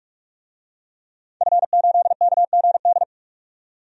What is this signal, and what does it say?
Morse code, heard as a keyed tone, about 41 wpm, 700 Hz. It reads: F8KGD
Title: DXCC-100 Call Signs - 5 Characters - 45wpm